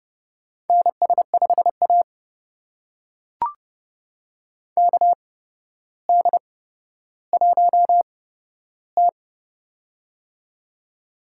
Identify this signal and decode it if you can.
Morse code, heard as a keyed tone, about 31 wpm, spaced as for 13 wpm, 700 Hz. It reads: NS5A KD1T
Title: USA Call Signs - 4 Characters - 8x Character Spacing - 30wpm